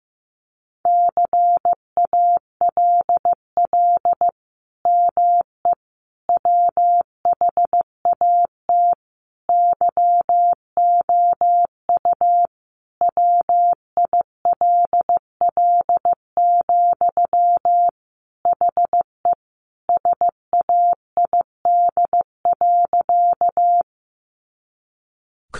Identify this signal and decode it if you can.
Morse code, heard as a keyed tone, about 15 wpm, 700 Hz. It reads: CALL ME WHAT YOU WILL, HE SAID.
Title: Sentences from Top 200 Words 15wpm